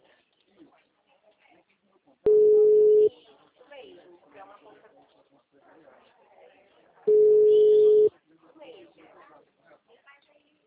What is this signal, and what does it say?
Ringback tone (Europe, 425 Hz), with speech.